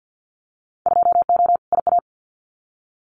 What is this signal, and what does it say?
Morse code, heard as a keyed tone, about 50 wpm, 700 Hz. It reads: 2O IA
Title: Winter Field Day Exchanges 50wpm